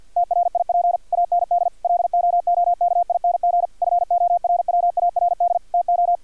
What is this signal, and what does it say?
Morse code, heard as a keyed tone, about 50 wpm, 700 Hz, with radio noise. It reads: TRIP AND LOOKING FORWARD TO
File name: archived / CRSnet